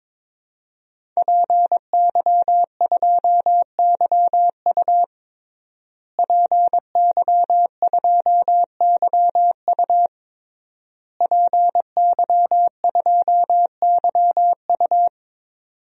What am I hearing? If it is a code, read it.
Morse code, heard as a keyed tone, about 22 wpm, 700 Hz. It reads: PY2YU PY2YU PY2YU
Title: DXCC-100 Call Signs - 5 Characters - Repeated 3x - 22wpm